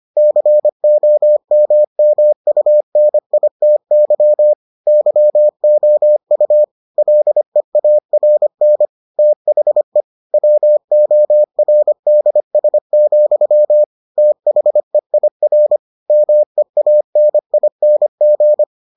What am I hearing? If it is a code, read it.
Morse code, heard as a keyed tone, about 25 wpm, 600 Hz. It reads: COMMUNITY YOU LEARN THE WORDS, THEIR MEANING